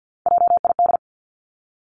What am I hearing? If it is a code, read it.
Morse code, heard as a keyed tone, about 50 wpm, 700 Hz. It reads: JID
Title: Three Challenging Letter-Number Combinations 50wpm